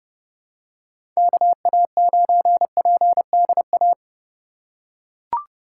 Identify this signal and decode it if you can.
Morse code, heard as a keyed tone, about 31 wpm, 700 Hz. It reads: KA9PDA
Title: USA Call Signs - 2x Character Spacing - 30wpm